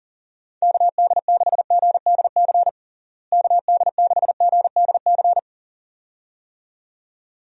Morse code, heard as a keyed tone, about 40 wpm, 700 Hz. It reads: KD6GDC KD6GDC
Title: USA Call Signs - 6 Characters - Repeated 2x - 40wpm